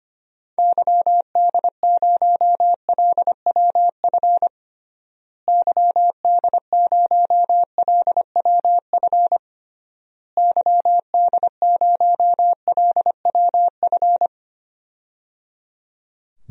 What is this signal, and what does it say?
Morse code, heard as a keyed tone, about 25 wpm, 700 Hz. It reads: YD0LWF YD0LWF YD0LWF
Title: DXCC-100 Call Signs - 6 Characters - Repeated 3x - 25wpm